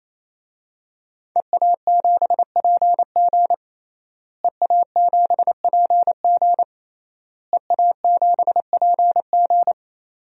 Morse code, heard as a keyed tone, about 28 wpm, 700 Hz. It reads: EA7PG EA7PG EA7PG
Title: DXCC-100 Call Signs - 5 Characters - Repeated 3x - 28wpm